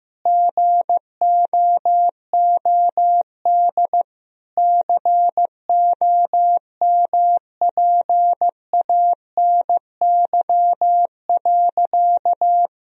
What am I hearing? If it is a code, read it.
Morse code, heard as a keyed tone, about 16 wpm, 700 Hz. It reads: GOOD COMPANY.
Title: Sentences from Top 1000 Words - Review Part 3 - 15wpm